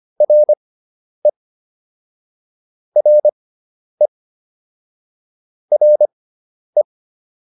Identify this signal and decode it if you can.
Morse code, heard as a keyed tone, about 25 wpm, spaced as for 10 wpm, 600 Hz. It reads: RE RE RE